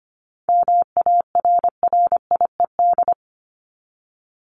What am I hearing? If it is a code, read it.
Morse code, heard as a keyed tone, about 25 wpm, 700 Hz. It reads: MARRIED
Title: Top 1501-1750 Words 25wpm